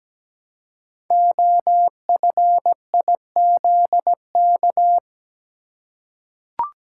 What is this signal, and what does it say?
Morse code, heard as a keyed tone, about 17 wpm, 700 Hz. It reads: OFIZK
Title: Code Groups - 5 Letters - 17wpm